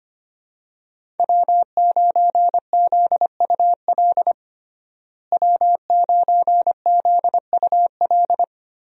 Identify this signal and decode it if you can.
Morse code, heard as a keyed tone, about 25 wpm, 700 Hz. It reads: W9ZUL W9ZUL
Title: USA Call Signs - 5 Characters - Repeated 3x - 25wpm